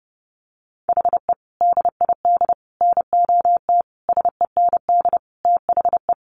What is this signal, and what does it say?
Morse code, heard as a keyed tone, about 30 wpm, 700 Hz. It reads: HE DID NOT SEND THE
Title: Sentences from Top 1500 Words 30wpm